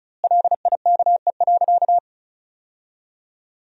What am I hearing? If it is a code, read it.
Morse code, heard as a keyed tone, about 36 wpm, 700 Hz. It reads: LIKE.
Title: 2x Word-Spacing - Sentences from Top 100 Words - 35wpm